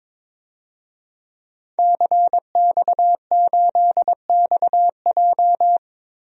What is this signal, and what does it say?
Morse code, heard as a keyed tone, about 21 wpm, 700 Hz. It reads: CX8XJ
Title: DXCC-100 Call Signs - 5 Characters - 22wpm